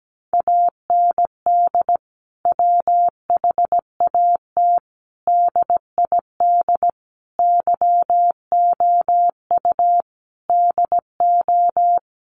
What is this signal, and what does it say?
Morse code, heard as a keyed tone, about 17 wpm, 700 Hz. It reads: AND WHAT DID YOU DO
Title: Sentences from Top 300 Words 17wpm